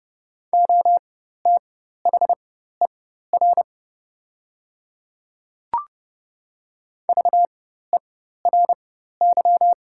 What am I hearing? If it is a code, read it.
Morse code, heard as a keyed tone, about 30 wpm, spaced as for 14 wpm, 700 Hz. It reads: OTHER VERY
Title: 4x Character Spacing - Top 200 Words - 30wpm